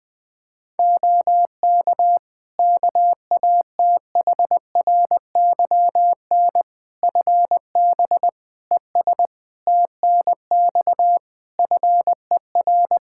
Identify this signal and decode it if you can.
Morse code, heard as a keyed tone, about 21 wpm, 700 Hz. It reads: OK KATHRYN FB ES TNX FER
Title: K1USN SST QSO Practice 20wpm